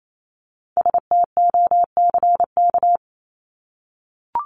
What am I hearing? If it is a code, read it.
Morse code, heard as a keyed tone, about 28 wpm, 700 Hz. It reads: STOCK
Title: Top 501-750 Words 28wpm